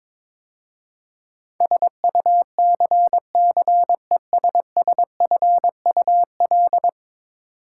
Morse code, heard as a keyed tone, about 22 wpm, 700 Hz. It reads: SUCCESSFUL
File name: practice / 22wpm / 10 Letter Words - First 3 Letters Given - 22wpm